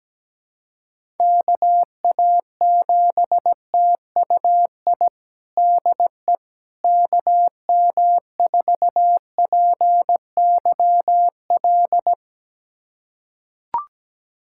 Morse code, heard as a keyed tone, about 17 wpm, 700 Hz. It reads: KA7TUI DE KM4PYL
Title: USA Callsign DE Callsign 17wpm